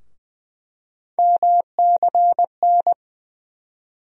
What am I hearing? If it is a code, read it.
Morse code, heard as a keyed tone, about 20 wpm, 700 Hz. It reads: MCN
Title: USA Call Sign Postfixes - 3 Letters 20wpm